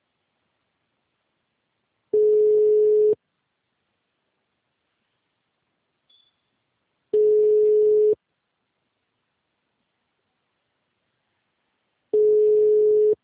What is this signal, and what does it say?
Ringback tone (Europe, 425 Hz).